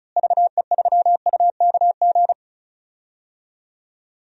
Morse code, heard as a keyed tone, about 34 wpm, 700 Hz. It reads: VE3UKG
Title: DXCC-100 Call Signs - 6 Characters - Repeated 3x - 35wpm